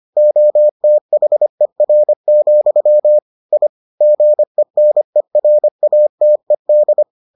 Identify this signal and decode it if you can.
Morse code, heard as a keyed tone, about 25 wpm, 600 Hz. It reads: OTHER, I GENERATED